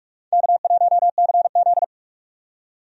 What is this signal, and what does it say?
Morse code, heard as a keyed tone, about 45 wpm, 700 Hz. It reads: K1CZ
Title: USA Call Signs - 4 Characters - 45wpm